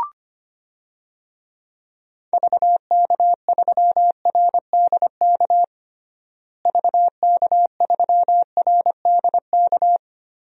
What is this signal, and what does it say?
Morse code, heard as a keyed tone, about 25 wpm, 700 Hz. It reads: VK3RDK VK3RDK
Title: DXCC-100 Call Signs - 6 Characters - Repeated 2x - 25wpm